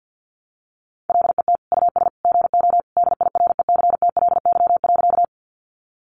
Morse code, heard as a keyed tone, about 50 wpm, 700 Hz. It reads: LET US GO DIRECTLY.
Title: Sentences from Top 800 Words 50wpm